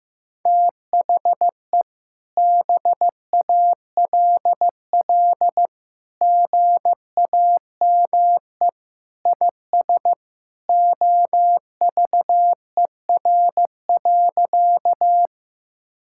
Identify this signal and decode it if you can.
Morse code, heard as a keyed tone, about 16 wpm, 700 Hz. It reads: THE BALL GAME IS OVER.
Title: Sentences from Top 1000 Words - Review Part 6 - 15wpm